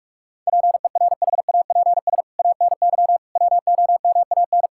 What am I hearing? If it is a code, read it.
Morse code, heard as a keyed tone, about 46 wpm, 700 Hz. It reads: PERHAPS ANY WOMAN
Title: Sentences from Top 1600 Words 45wpm